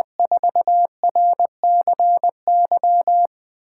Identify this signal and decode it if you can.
Morse code, heard as a keyed tone, about 20 wpm, 700 Hz. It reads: 4RCY
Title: USA Callsign DE Callsign 20wpm